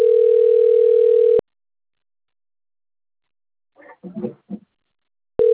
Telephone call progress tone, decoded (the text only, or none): ringback tone